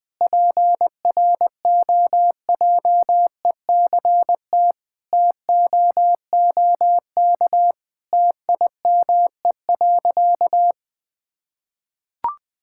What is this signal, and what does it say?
Morse code, heard as a keyed tone, about 20 wpm, 700 Hz. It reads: PROJECT TOOK TIME.
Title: Sentences with 3-Word Phrases - Spoken Beginning - 20wpm